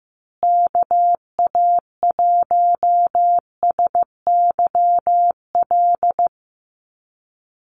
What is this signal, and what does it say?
Morse code, heard as a keyed tone, about 15 wpm, 700 Hz. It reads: KA1SYL
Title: USA Call Signs Repeated 3x 15wpm